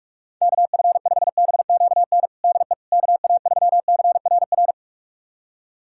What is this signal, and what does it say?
Morse code, heard as a keyed tone, about 45 wpm, 700 Hz. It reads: KF5BQN DE KA3CRR
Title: USA Callsign DE Callsign - 6 Characters - 45wpm